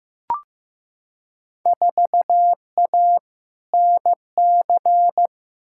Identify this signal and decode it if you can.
Morse code, heard as a keyed tone, about 15 wpm, 700 Hz. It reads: 4A NC
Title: Field Day Exchanges - 15wpm